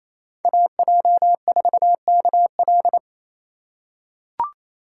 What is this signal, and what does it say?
Morse code, heard as a keyed tone, about 28 wpm, 700 Hz. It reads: AJ4KL
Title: USA Call Signs - 5 Characters - 2x Character Spacing - 28wpm